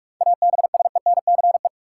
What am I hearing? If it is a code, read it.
Morse code, heard as a keyed tone, about 44 wpm, 700 Hz. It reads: ABSENCE
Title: Sets of 4 Words - New Word Drill - 45wpm